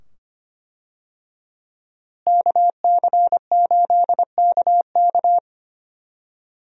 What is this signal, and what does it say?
Morse code, heard as a keyed tone, about 25 wpm, 700 Hz. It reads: KC8KK
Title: USA Call Signs - 5 Characters - Repeated 2x - 25wpm